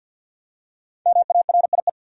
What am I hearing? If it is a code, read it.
Morse code, heard as a keyed tone, about 50 wpm, 700 Hz. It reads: MARIE
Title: USA Common Names - Max 5 Letters - 50wpm